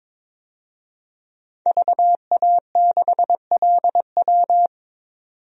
Morse code, heard as a keyed tone, about 22 wpm, 700 Hz. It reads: VA6LW
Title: DXCC-100 Call Signs - 5 Characters - 22wpm